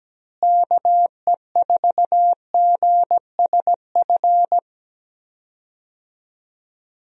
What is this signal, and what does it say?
Morse code, heard as a keyed tone, about 17 wpm, 700 Hz. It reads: KE4GSF
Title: USA Call Signs - 6 Characters - Repeated 2x - 17wpm